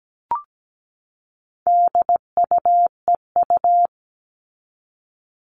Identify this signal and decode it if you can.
Morse code, heard as a keyed tone, about 17 wpm, 700 Hz. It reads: DUEU
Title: All Cut Numbers - 4-Digit 17wpm